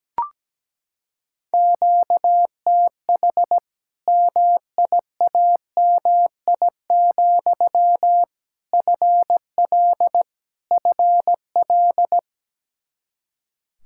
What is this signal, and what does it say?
Morse code, heard as a keyed tone, about 17 wpm, 700 Hz. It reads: QTH MIAMI, FL FL
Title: Practice QSO - Exchange 1 - Beginning - 17wpm